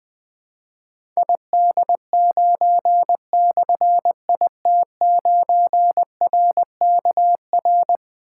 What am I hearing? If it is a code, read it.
Morse code, heard as a keyed tone, about 20 wpm, 700 Hz. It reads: ID9/IT9RKR
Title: Difficult Call Signs 20wpm